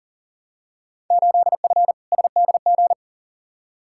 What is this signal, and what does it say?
Morse code, heard as a keyed tone, about 40 wpm, 700 Hz. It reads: 8F SDG
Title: Field Day Exchanges Encore - 40wpm